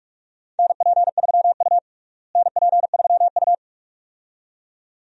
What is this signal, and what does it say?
Morse code, heard as a keyed tone, about 44 wpm, 700 Hz. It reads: NP3U NP3U
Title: USA Call Signs - 4 Characters - Repeated 2x - 45wpm